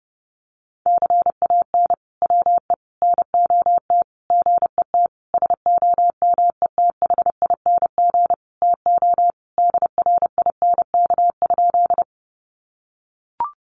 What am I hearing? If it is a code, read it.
Morse code, heard as a keyed tone, about 30 wpm, 700 Hz. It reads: CAN WE NOT GET SOMETHING TO DRINK?
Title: Sentences from Top 1500 Words 30wpm